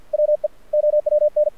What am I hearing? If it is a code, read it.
Morse code, heard as a keyed tone, about 52 wpm, 600 Hz. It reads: WE OWN
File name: archived / CRSnet